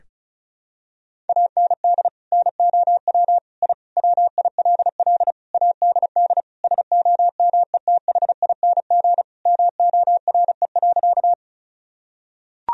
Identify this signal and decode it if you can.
Morse code, heard as a keyed tone, about 34 wpm, 700 Hz. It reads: AND NOW I WILL ADD SOMETHING MORE.
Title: Sentences from Top 1000 Words - Review Part 1 - 35wpm